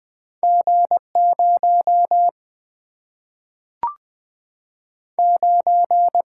Morse code, heard as a keyed tone, about 20 wpm, 700 Hz. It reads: G0 9
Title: Single-Letter Single-Number 20wpm